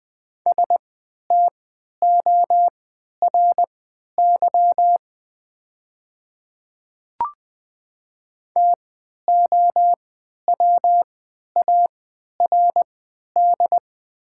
Morse code, heard as a keyed tone, about 20 wpm, spaced as for 11 wpm, 700 Hz. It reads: STORY TOWARD
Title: 3x Character Spacing - Top 500 Words - 20wpm